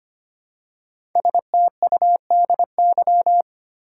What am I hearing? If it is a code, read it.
Morse code, heard as a keyed tone, about 25 wpm, 700 Hz. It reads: STUDY
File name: 5 Letter Words - First 3 Letters Given - 25wpm